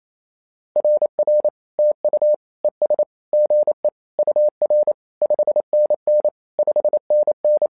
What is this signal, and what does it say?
Morse code, heard as a keyed tone, about 28 wpm, 600 Hz. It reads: RR TU ES GE UR 5NN 5NN
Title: SOTA QSO Practice 28wpm